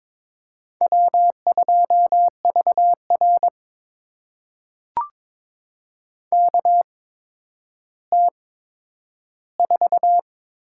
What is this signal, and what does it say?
Morse code, heard as a keyed tone, about 22 wpm, spaced as for 11 wpm, 700 Hz. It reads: W2VR KT4
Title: USA Call Signs - 4 Characters - 8x Character Spacing - 22wpm